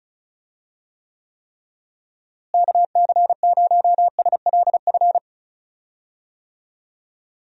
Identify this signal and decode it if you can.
Morse code, heard as a keyed tone, about 36 wpm, 700 Hz. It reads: KC0SLF
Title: USA Call Signs - 6 Characters - Repeated 3x - 35wpm